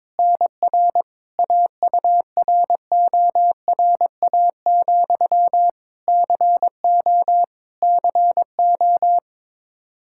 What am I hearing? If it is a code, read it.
Morse code, heard as a keyed tone, about 22 wpm, 700 Hz. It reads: NR AURORA, CO CO
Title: K1USN SST QSO Practice 22wpm